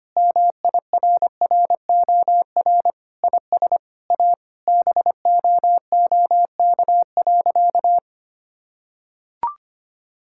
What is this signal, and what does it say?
Morse code, heard as a keyed tone, about 25 wpm, 700 Hz. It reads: MIRROR IS A BOOK.
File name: practice / 25wpm / Sentences with 4-Word Phrases - 25wpm